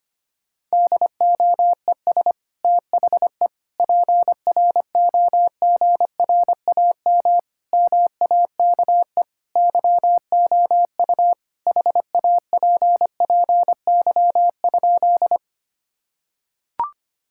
Morse code, heard as a keyed tone, about 25 wpm, 700 Hz. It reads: DOES THE PROGRAM MAKE YOU HAPPY?
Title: Sentences from Top 1000 Words - Review Part 5 - 25wpm